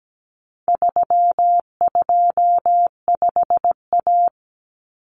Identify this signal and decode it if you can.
Morse code, heard as a keyed tone, about 17 wpm, 700 Hz. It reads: 325A
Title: Common Cut Numbers - 4-Digit 17wpm